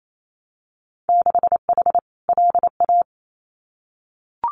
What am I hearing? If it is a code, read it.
Morse code, heard as a keyed tone, about 28 wpm, 700 Hz. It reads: 6H LA
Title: Winter Field Day Exchanges Encore 28wpm